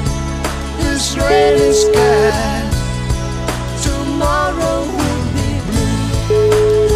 Ringback tone (Europe, 425 Hz), with noise.